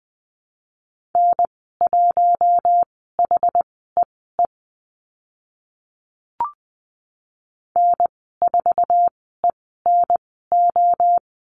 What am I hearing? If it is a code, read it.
Morse code, heard as a keyed tone, about 20 wpm, spaced as for 15 wpm, 700 Hz. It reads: N1HEE N4ENO
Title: 062 Lesson 11g - Letter h - Call Signs 20wpm